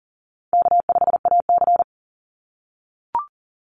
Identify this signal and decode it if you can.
Morse code, heard as a keyed tone, about 40 wpm, 700 Hz. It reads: K5AC
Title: USA Call Signs 40wpm